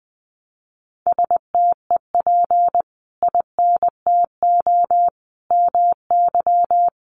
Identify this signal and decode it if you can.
Morse code, heard as a keyed tone, about 20 wpm, 700 Hz. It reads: STEP INTO MY
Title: Sentences from Top 600 Words 20wpm